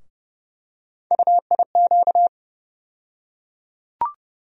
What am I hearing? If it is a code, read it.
Morse code, heard as a keyed tone, about 30 wpm, 700 Hz. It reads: UIQ
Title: USA Call Sign Postfixes - 3 Letters 30wpm